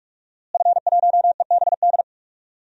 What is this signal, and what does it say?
Morse code, heard as a keyed tone, about 45 wpm, 700 Hz. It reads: F1EBD
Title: DXCC-100 Call Signs - 5 Characters - Repeated 3x - 45wpm